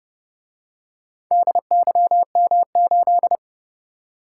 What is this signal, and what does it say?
Morse code, heard as a keyed tone, about 28 wpm, 700 Hz. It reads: DYM8